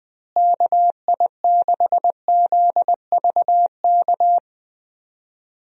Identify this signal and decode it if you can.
Morse code, heard as a keyed tone, about 20 wpm, 700 Hz. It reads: KI6ZVK
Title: USA Call Signs - 6 Characters - Repeated 3x - 20wpm